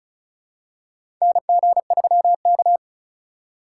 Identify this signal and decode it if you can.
Morse code, heard as a keyed tone, about 36 wpm, 700 Hz. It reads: NG3K
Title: USA Call Signs - 4x Character Spacing - 35wpm